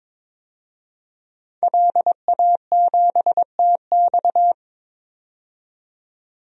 Morse code, heard as a keyed tone, about 22 wpm, 700 Hz. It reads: LA7TX
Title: DXCC-100 Call Signs - 5 Characters - 22wpm